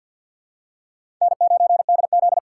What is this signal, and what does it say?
Morse code, heard as a keyed tone, about 51 wpm, 700 Hz. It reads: N9DZ
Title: USA Call Signs - 4 Characters - 2x Character Spacing - 50wpm